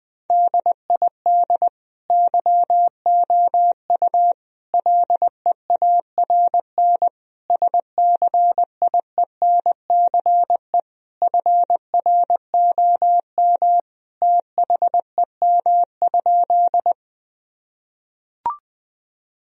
Morse code, heard as a keyed tone, about 21 wpm, 700 Hz. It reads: DID YOU LEARN SCIENCE FROM THEM?
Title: Sentences from Top 1000 Words - Review Part 2 - 20wpm